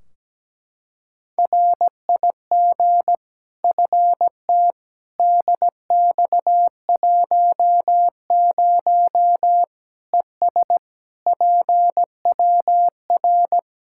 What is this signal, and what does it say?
Morse code, heard as a keyed tone, about 17 wpm, 700 Hz. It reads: RIG FT DX10 ES PWR
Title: Practice QSO - Exchange 2 - Info - 17wpm